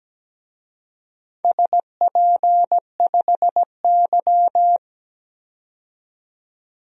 Morse code, heard as a keyed tone, about 17 wpm, 700 Hz. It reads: SP5Y
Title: DXCC-100 Call Signs - 4 Characters - 17wpm